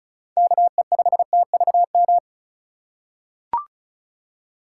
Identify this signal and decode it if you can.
Morse code, heard as a keyed tone, about 34 wpm, 700 Hz. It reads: KE5TVM
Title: USA Call Signs - 6 Characters - 4x Character Spacing - 35wpm